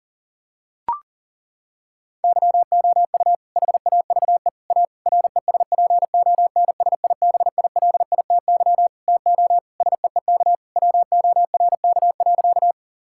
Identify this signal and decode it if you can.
Morse code, heard as a keyed tone, about 40 wpm, 700 Hz. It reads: YOU HAVE A RESPONSIBILITY TO SEEK WORK.
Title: Sentences from Top 1600 Words 40wpm